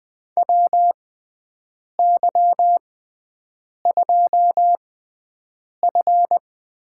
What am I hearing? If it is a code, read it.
Morse code, heard as a keyed tone, about 20 wpm, spaced as for 7 wpm, 700 Hz. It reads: WY2F
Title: USA Call Signs - 6x Character Spacing - 20wpm